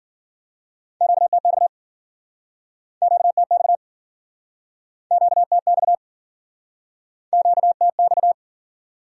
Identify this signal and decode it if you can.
Morse code, heard as a keyed tone, about 51 wpm, 700 Hz. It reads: QTX QTX QTX QTX